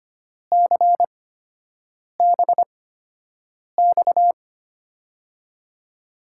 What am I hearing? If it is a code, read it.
Morse code, heard as a keyed tone, about 25 wpm, spaced as for 7 wpm, 700 Hz. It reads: CBX
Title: USA Call Signs - 8x Character Spacing - 25wpm